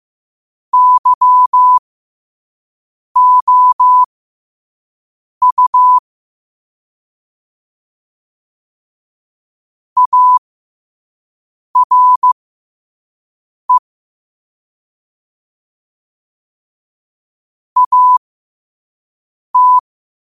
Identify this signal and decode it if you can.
Morse code, heard as a keyed tone, about 15 wpm, spaced as for 5 wpm, 1000 Hz. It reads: YOU ARE AT
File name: qotdfiles / 5wpm